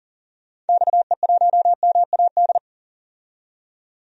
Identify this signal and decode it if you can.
Morse code, heard as a keyed tone, about 40 wpm, 700 Hz. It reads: XE1MAD
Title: DXCC-100 Call Signs - 6 Characters - 40wpm